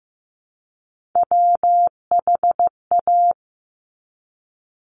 Morse code, heard as a keyed tone, about 15 wpm, 700 Hz. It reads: WHA
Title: Top Three Letter Combinations in Words 15wpm